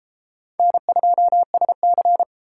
Morse code, heard as a keyed tone, about 33 wpm, 700 Hz. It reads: N2SC